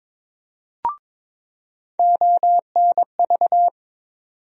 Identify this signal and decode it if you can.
Morse code, heard as a keyed tone, about 22 wpm, 700 Hz. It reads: ONV
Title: USA Call Sign Postfixes - 3 Letters 22wpm